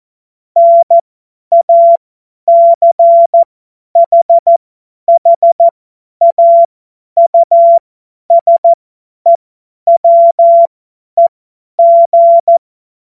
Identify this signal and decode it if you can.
Morse code, heard as a keyed tone, about 14 wpm, spaced as for 10 wpm, 680 Hz. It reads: NACHHAUSEWEG
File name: Witz-21-Oktober-2025-50-F70